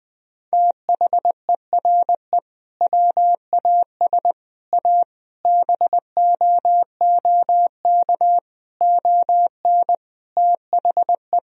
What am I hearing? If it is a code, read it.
Morse code, heard as a keyed tone, about 20 wpm, 700 Hz. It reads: THERE WAS A BOOK ON THE
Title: Sentences from Top 1000 Words - Review Part 3 - 20wpm